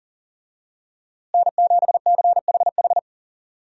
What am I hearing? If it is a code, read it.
Morse code, heard as a keyed tone, about 37 wpm, 700 Hz. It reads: N7CHH
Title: USA Call Signs - 5 Characters - Repeated 2x - 40wpm